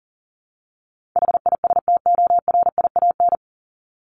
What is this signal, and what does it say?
Morse code, heard as a keyed tone, about 40 wpm, 700 Hz. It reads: HISTORIAN
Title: Top 1751-2000 Words 40wpm